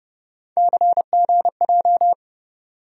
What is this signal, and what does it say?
Morse code, heard as a keyed tone, about 32 wpm, 700 Hz. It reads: CGJ
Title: USA Call Sign Postfixes - 3 Letters 30wpm